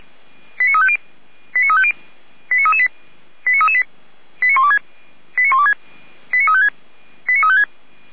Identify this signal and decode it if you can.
Five-tone selective call (ZVEI), heard as quick runs of five tones, tones of about 70 ms. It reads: 89350 89350 89338 89338 89126 89126 89356 89356